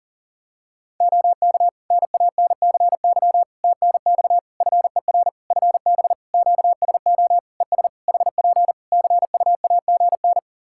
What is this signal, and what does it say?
Morse code, heard as a keyed tone, about 40 wpm, 700 Hz. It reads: OK NANCY TNX FER FB QSO ES HP CUAGN
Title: Practice QSO - Exchange 4 - Ending - 40wpm